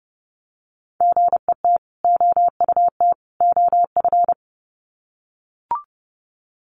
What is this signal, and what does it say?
Morse code, heard as a keyed tone, about 30 wpm, 700 Hz. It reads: GET OUT OF
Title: Sets of 3 Words 30wpm